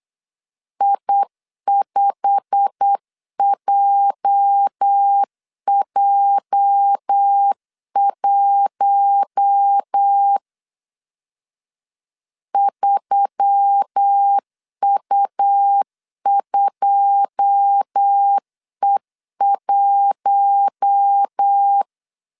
Morse code, heard as a keyed tone, about 8 wpm, 800 Hz. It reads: I5JJ1 3U2E1
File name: CW / DET / 1 9 1500C S C 8wpm 5653